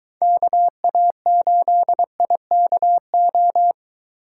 Morse code, heard as a keyed tone, about 23 wpm, 700 Hz. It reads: KA8IKO